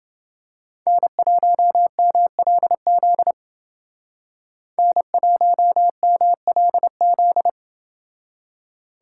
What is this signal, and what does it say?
Morse code, heard as a keyed tone, about 29 wpm, 700 Hz. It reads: N1MLZ N1MLZ